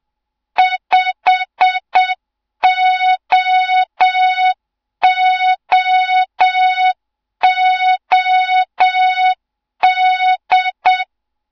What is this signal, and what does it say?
Morse code, heard as a keyed tone, about 7 wpm, 3700 Hz. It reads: 5OOOD
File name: Morze-HA2MN / hang / 07 GYAKORLAT